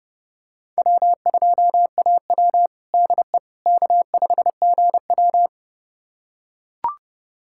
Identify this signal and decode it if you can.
Morse code, heard as a keyed tone, about 30 wpm, 700 Hz. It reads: W2AW DE K5GW
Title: USA Callsign DE Callsign - 4 Characters - 30wpm